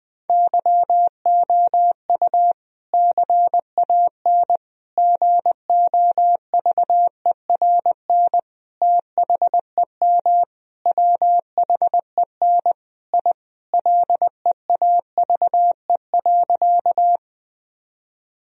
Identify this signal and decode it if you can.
Morse code, heard as a keyed tone, about 20 wpm, 700 Hz. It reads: YOU CAN GOVERN THEM WHEN I LEAVE.
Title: Sentences from Top 1000 Words - Review Part 2 - 20wpm